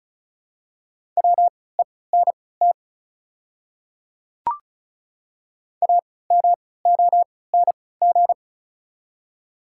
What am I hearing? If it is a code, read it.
Morse code, heard as a keyed tone, about 34 wpm, spaced as for 20 wpm, 690 Hz. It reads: WENT AMONG
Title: 3x Character Spacing - Top 500 Words - 35wpm